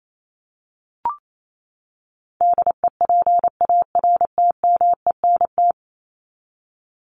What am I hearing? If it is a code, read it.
Morse code, heard as a keyed tone, about 28 wpm, 700 Hz. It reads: DEPARTMENT
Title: Top 751-1000 Words 28wpm